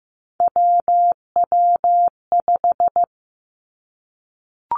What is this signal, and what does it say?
Morse code, heard as a keyed tone, about 15 wpm, 700 Hz. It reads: WW5
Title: Three Challenging Letter-Number Combinations 15wpm